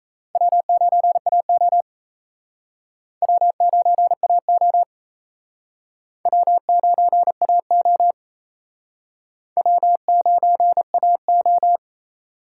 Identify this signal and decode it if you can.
Morse code, heard as a keyed tone, about 34 wpm, 700 Hz. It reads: W9AO W9AO W9AO W9AO
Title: Speed Racing - USA 1x2 Call Signs 28wpm